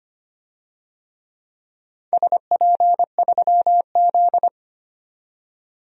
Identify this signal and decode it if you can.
Morse code, heard as a keyed tone, about 25 wpm, 700 Hz. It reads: SP3Z
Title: DXCC-100 Call Signs - 4 Characters - 25wpm